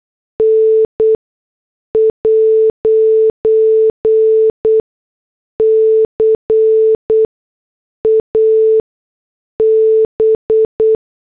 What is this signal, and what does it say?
Morse code, heard as a keyed tone, about 8 wpm, spaced as for 6 wpm, 440 Hz. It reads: N'CAB
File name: morse / headlines